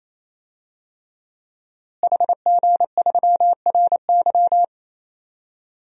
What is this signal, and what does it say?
Morse code, heard as a keyed tone, about 27 wpm, 700 Hz. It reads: HG3RY
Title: DXCC-100 Call Signs - 5 Characters - Repeated 3x - 28wpm